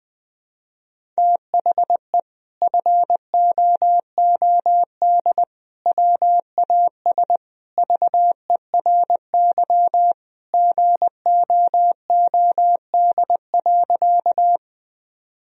Morse code, heard as a keyed tone, about 20 wpm, 700 Hz. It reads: THE FOOD WAS VERY GOOD.
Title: Sentences from Top 1000 Words - Review Part 1 - 20wpm